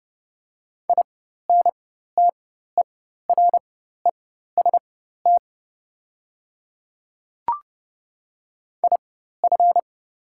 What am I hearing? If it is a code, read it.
Morse code, heard as a keyed tone, about 31 wpm, spaced as for 14 wpm, 700 Hz. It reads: INTEREST IF
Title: 4x Character Spacing - Top 500 Words - 30wpm